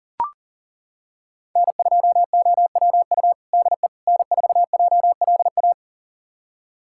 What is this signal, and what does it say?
Morse code, heard as a keyed tone, about 40 wpm, 700 Hz. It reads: N2OWU DE N4JLA
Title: USA Callsign DE Callsign - 5 Characters - 40wpm